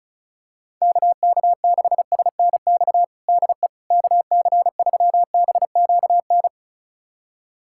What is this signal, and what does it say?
Morse code, heard as a keyed tone, about 36 wpm, 700 Hz. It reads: KK6SNX DE KC3BQN
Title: USA Callsign DE Callsign - 6 Characters - 35wpm